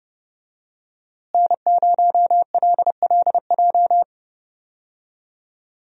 Morse code, heard as a keyed tone, about 30 wpm, 700 Hz. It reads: N0LLJ
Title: USA Call Signs - 5 Characters - Repeated 2x - 30wpm